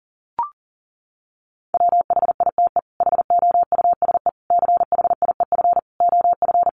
Morse code, heard as a keyed tone, about 40 wpm, 700 Hz. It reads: WHITE HOUSE CHIEF OF
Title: Sets of 5 Words Encore 40wpm